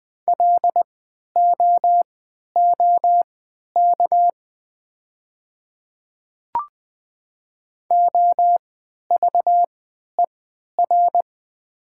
Morse code, heard as a keyed tone, about 20 wpm, spaced as for 11 wpm, 700 Hz. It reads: LOOK OVER
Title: 3x Character Spacing - Top 100 Words - 20wpm